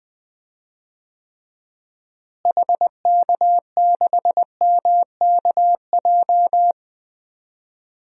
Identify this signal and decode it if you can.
Morse code, heard as a keyed tone, about 21 wpm, 700 Hz. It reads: HK6MKJ
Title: DXCC-100 Call Signs - 6 Characters - Repeated 3x - 20wpm